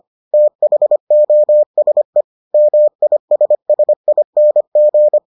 Morse code, heard as a keyed tone, about 25 wpm, 600 Hz. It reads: THOSE MISSING